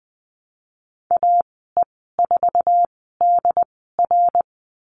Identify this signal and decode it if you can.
Morse code, heard as a keyed tone, about 20 wpm, spaced as for 15 wpm, 700 Hz. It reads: AE4DR
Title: 088 Lesson 15g - Number 5 - Call Signs 20wpm